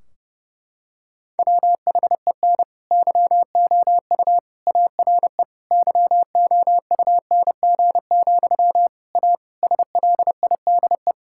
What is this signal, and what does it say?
Morse code, heard as a keyed tone, about 32 wpm, 700 Hz. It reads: WHEN YOU ARE YOUNG, A SLIDE
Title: Sentences from Top 1600 Words 30wpm